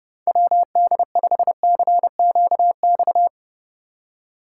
Morse code, heard as a keyed tone, about 28 wpm, 700 Hz. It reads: WD5CQX